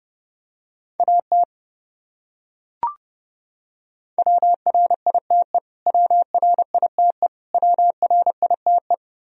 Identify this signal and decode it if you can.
Morse code, heard as a keyed tone, about 30 wpm, 700 Hz. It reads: AT WRITE WRITE WRITE
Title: Top 100 Words Repeated 30wpm